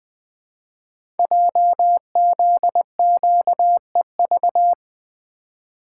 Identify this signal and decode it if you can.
Morse code, heard as a keyed tone, about 20 wpm, 700 Hz. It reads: JZQEV
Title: Code Groups - 5 Letters - 20wpm